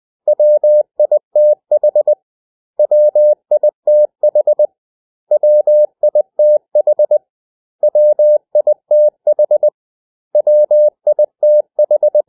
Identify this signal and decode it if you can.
Morse code, heard as a keyed tone, about 20 wpm, 600 Hz. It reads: WITH WITH WITH WITH WITH
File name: cw / 100 Common English Words